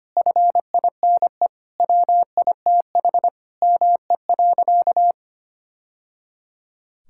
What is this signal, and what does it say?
Morse code, heard as a keyed tone, about 25 wpm, 700 Hz. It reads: FINE WITH ME.
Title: Sentences from Top 1000 Words - Review Part 5 - 25wpm